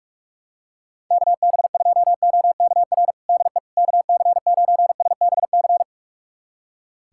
Morse code, heard as a keyed tone, about 46 wpm, 700 Hz. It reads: KB2OKR DE KC9SBC
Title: USA Callsign DE Callsign - 6 Characters - 45wpm